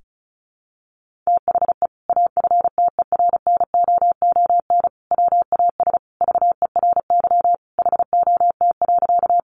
Morse code, heard as a keyed tone, about 34 wpm, 700 Hz. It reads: THE AFTERNOON WAS VERY HOT.
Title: Sentences from Top 1500 Words 35wpm